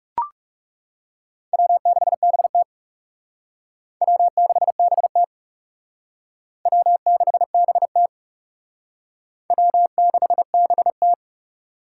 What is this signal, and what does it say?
Morse code, heard as a keyed tone, about 37 wpm, 700 Hz. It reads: W6BT W6BT W6BT W6BT
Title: Speed Racing - USA 1x2 Call Signs 30wpm